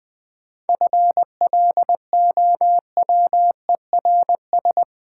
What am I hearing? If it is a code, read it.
Morse code, heard as a keyed tone, about 22 wpm, 700 Hz. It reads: FLOWERS
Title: Easy Words - Speed-Racing with Hints - 20wpm